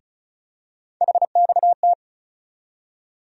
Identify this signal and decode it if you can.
Morse code, heard as a keyed tone, about 36 wpm, 700 Hz. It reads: HXT
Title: USA Call Sign Postfixes - 3 Letters 35wpm